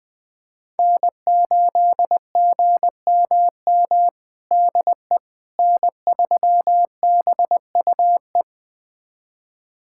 Morse code, heard as a keyed tone, about 21 wpm, 700 Hz. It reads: N8GMM DE N3BUE
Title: USA Callsign DE Callsign - 5 Characters - 20wpm